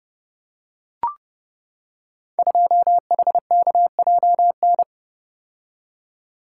Morse code, heard as a keyed tone, about 30 wpm, 700 Hz. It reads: 2HKJN
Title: Code Groups - 5 Letter:Numbers - 30wpm